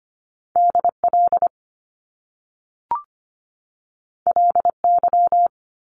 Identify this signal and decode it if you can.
Morse code, heard as a keyed tone, about 25 wpm, 700 Hz. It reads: DL LY
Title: Top Two Letter Combinations in Words 25wpm